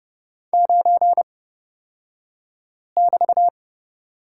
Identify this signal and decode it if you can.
Morse code, heard as a keyed tone, about 31 wpm, spaced as for 5 wpm, 700 Hz. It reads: 9<BT>
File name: Lesson 36d - BT - Two Characters 30wpm